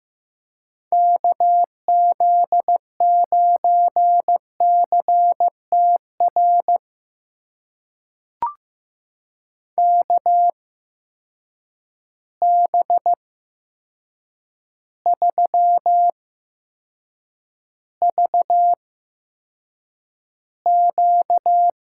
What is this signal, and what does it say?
Morse code, heard as a keyed tone, about 15 wpm, spaced as for 7 wpm, 700 Hz. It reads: KZ9CTR KB3VQ
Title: USA Call Signs - 6 Characters - 8x Character Spacing - 15wpm